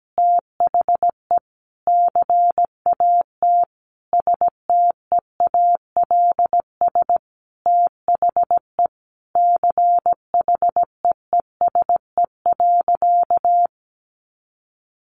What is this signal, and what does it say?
Morse code, heard as a keyed tone, about 17 wpm, 700 Hz. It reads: THE CAT STEALS THE CHEESE.
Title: Easy Phrases 17wpm